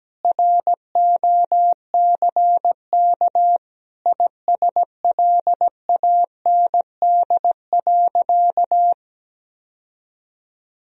Morse code, heard as a keyed tone, about 17 wpm, 700 Hz. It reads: ROCK ISLAND.
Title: Sentences from Top 1000 Words - Review Part 3 - 17wpm